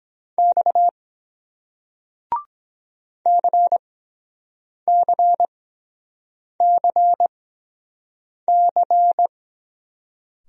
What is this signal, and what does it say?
Morse code, heard as a keyed tone, about 23 wpm, spaced as for 7 wpm, 700 Hz. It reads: X CCCC